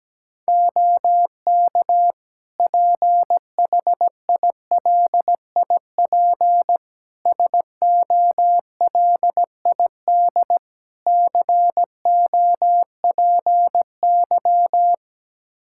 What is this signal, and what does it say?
Morse code, heard as a keyed tone, about 17 wpm, 700 Hz. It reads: OK PHILIP SOLID COPY
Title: K1USN SST QSO Practice 17wpm